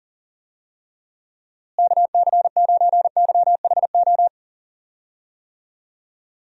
Morse code, heard as a keyed tone, about 40 wpm, 700 Hz. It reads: KC9YHO
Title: USA Call Signs - 6 Characters - Repeated 2x - 40wpm